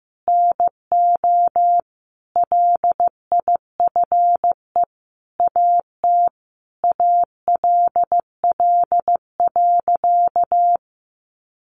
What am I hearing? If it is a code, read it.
Morse code, heard as a keyed tone, about 15 wpm, 700 Hz. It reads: NO LIFE AT ALL.
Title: Sentences from Top 1100 Words 15wpm